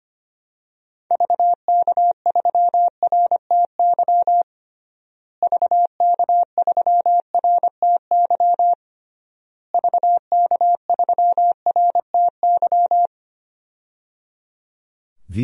Morse code, heard as a keyed tone, about 25 wpm, 700 Hz. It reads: VK3RTY VK3RTY VK3RTY
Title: DXCC-100 Call Signs - 6 Characters - Repeated 3x - 25wpm